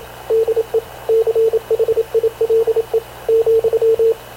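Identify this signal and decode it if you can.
Morse code, heard as a keyed tone, about 27 wpm, 450 Hz, with radio noise. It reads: DE CHILE ,
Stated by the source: Havana Radio / CLA sending Spanish press (PX) in Morse, recorded by Radio Officer Paul Dumesnil onboard CNAV QUEST/CZDO.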